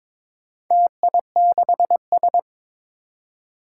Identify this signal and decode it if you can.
Morse code, heard as a keyed tone, about 22 wpm, 700 Hz. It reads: TI6S
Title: Four Challenging Letter-Number Combinations 22wpm